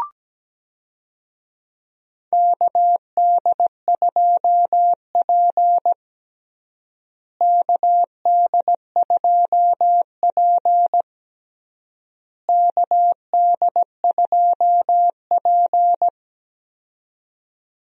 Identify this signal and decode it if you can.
Morse code, heard as a keyed tone, about 17 wpm, 700 Hz. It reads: KD2P KD2P KD2P
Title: USA Call Signs - 4 Characters - Repeated 3x - 17wpm